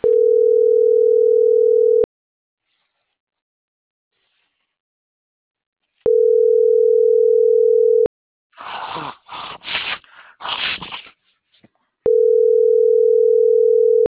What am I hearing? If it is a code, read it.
Ringback tone (North America), 440 + 480 Hz.